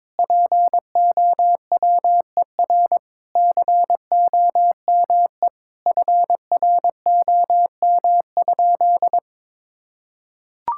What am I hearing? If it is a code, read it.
Morse code, heard as a keyed tone, about 22 wpm, 700 Hz. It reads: POWER COME FROM?
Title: Sentences from Top 1000 Words - Review Part 6 - 22wpm